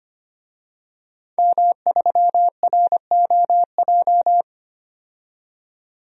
Morse code, heard as a keyed tone, about 25 wpm, 700 Hz. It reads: M3ROJ
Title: DXCC-100 Call Signs - 5 Characters - 25wpm